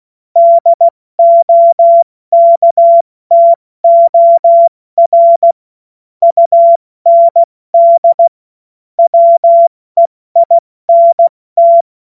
Morse code, heard as a keyed tone, about 16 wpm, 680 Hz. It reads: DOKTOR UND WEINT